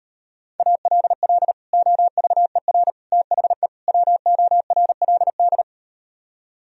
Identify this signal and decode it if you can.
Morse code, heard as a keyed tone, about 38 wpm, 700 Hz. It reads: ALL OVER THE WORLD